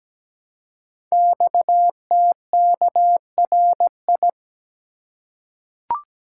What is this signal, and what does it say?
Morse code, heard as a keyed tone, about 17 wpm, 700 Hz. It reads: XTKRI
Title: Code Groups - 5 Letters - 17wpm